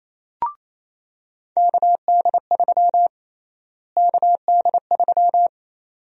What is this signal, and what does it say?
Morse code, heard as a keyed tone, about 28 wpm, 700 Hz. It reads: KD3 KD3
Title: USA Call Sign Prefixes - Repeated 2x - 28wpm